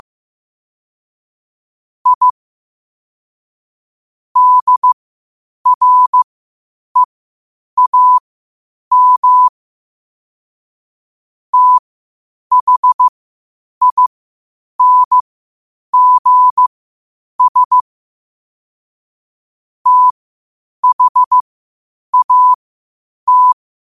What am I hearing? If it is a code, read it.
Morse code, heard as a keyed tone, about 15 wpm, spaced as for 8 wpm, 1000 Hz. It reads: I DREAM THINGS THAT
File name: qotdfiles / 8wpm